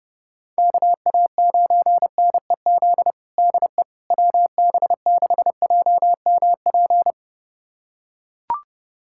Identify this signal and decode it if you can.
Morse code, heard as a keyed tone, about 30 wpm, 690 Hz. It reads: KA9NEZ DE WB6JMP
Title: USA Callsign DE Callsign 30wpm